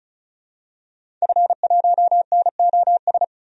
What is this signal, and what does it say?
Morse code, heard as a keyed tone, about 36 wpm, 700 Hz. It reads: F1NOS
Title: DXCC-100 Call Signs - 5 Characters - 35wpm